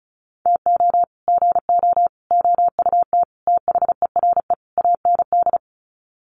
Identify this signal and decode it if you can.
Morse code, heard as a keyed tone, about 34 wpm, 700 Hz. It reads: TO GO OUT THERE AND
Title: Sets of 5 Words Encore2 35wpm